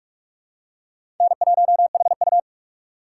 Morse code, heard as a keyed tone, about 47 wpm, 700 Hz. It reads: N1HU
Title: USA Call Signs - 4 Characters - 6x Character Spacing - 45wpm